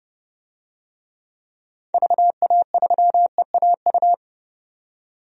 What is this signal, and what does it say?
Morse code, heard as a keyed tone, about 32 wpm, 700 Hz. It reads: VA3EAU
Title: DXCC-100 Call Signs - 6 Characters - 30wpm